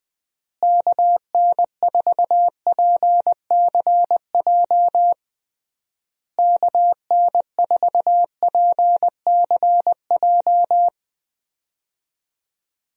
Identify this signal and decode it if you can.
Morse code, heard as a keyed tone, about 20 wpm, 700 Hz. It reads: KN4PCJ KN4PCJ
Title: USA Call Signs - 6 Characters - Repeated 3x - 20wpm